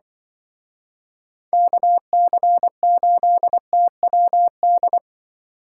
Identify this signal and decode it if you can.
Morse code, heard as a keyed tone, about 23 wpm, 700 Hz. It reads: KC8TWD
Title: Speed Racing - USA 2x3 Call Signs 20wpm